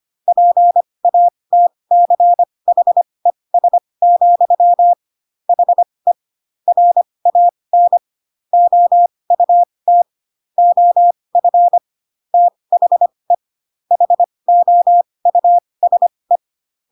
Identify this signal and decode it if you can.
Morse code, heard as a keyed tone, about 25 wpm, spaced as for 20 wpm, 700 Hz. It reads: PATCHES, HE RAN OUT OF THE HOUSE
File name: Pinocchio / Pinocchio ch8 15 25 ik1hge